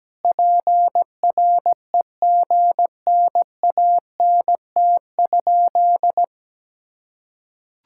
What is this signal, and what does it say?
Morse code, heard as a keyed tone, about 17 wpm, 700 Hz. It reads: PREGNANT?
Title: Sentences from Top 1000 Words - Review Part 6 - 17wpm